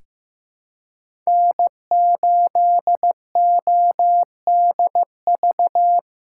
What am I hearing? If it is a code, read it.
Morse code, heard as a keyed tone, about 15 wpm, 700 Hz. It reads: N8ODV
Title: USA Call Signs - 2x Character Spacing - 15wpm